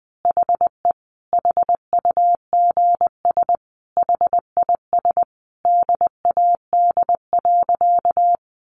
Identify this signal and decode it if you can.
Morse code, heard as a keyed tone, about 20 wpm, 700 Hz. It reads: HE HUGS HIS DAD.
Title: Easy Phrases 20wpm